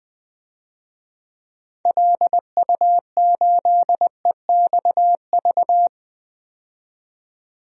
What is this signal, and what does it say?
Morse code, heard as a keyed tone, about 20 wpm, 700 Hz. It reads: LU8EXV
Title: DXCC-100 Call Signs - 6 Characters - 20wpm